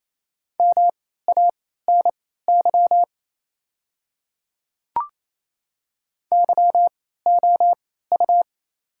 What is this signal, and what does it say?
Morse code, heard as a keyed tone, about 28 wpm, spaced as for 16 wpm, 700 Hz. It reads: MANY YOU
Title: 3x Character Spacing - Top 100 Words - 28wpm